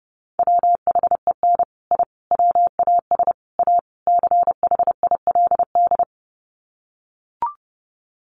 Morse code, heard as a keyed tone, about 30 wpm, 700 Hz. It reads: WHEN I WAS A CHILD
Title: Sets of 5 Words Encore 30wpm